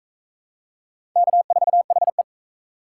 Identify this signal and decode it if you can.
Morse code, heard as a keyed tone, about 37 wpm, 700 Hz. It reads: K4HE